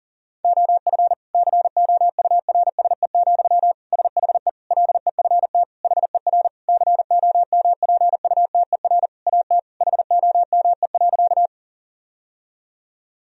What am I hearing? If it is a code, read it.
Morse code, heard as a keyed tone, about 37 wpm, 700 Hz. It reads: OF COURSE, SHE LEFT HER COMPUTER AT HOME.
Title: Sentences from Top 1000 Words - Review Part 3 - 40wpm